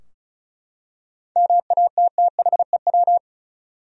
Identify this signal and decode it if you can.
Morse code, heard as a keyed tone, about 36 wpm, 700 Hz. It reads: MATTHEW
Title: USA Common Names - 35wpm